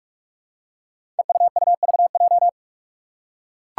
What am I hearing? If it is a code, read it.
Morse code, heard as a keyed tone, about 46 wpm, 700 Hz. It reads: EUUVJ
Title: Five Challenging Letter-Number Combinations 45wpm